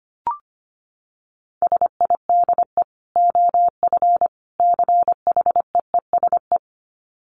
Morse code, heard as a keyed tone, about 25 wpm, 700 Hz. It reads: SIDE OF CHEESE
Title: Dit Challenge 25wpm